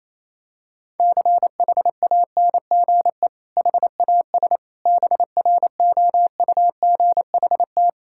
Morse code, heard as a keyed tone, about 28 wpm, 700 Hz. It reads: CHANGE HAS BROUGHT
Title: Sentences from Top 1600 Words 28wpm